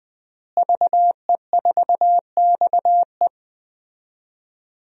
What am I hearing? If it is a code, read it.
Morse code, heard as a keyed tone, about 20 wpm, 700 Hz. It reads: VE4XE
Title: DXCC-100 Call Signs - 5 Characters - Repeated 2x - 20wpm